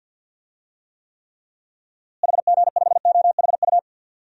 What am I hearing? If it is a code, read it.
Morse code, heard as a keyed tone, about 51 wpm, 700 Hz. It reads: HG5OHU
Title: DXCC-100 Call Signs - 6 Characters - Repeated 3x - 50wpm